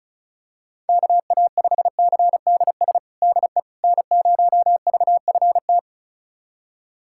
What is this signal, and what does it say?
Morse code, heard as a keyed tone, about 34 wpm, 700 Hz. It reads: KA5CDS DE N0VFT
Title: USA Callsign DE Callsign 35wpm